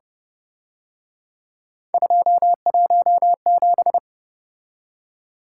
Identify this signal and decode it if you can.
Morse code, heard as a keyed tone, about 32 wpm, 700 Hz. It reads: 217